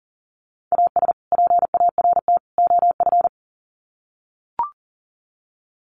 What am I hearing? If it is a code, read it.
Morse code, heard as a keyed tone, about 40 wpm, 700 Hz. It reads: AS PART OF
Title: Sets of 3 Words 40wpm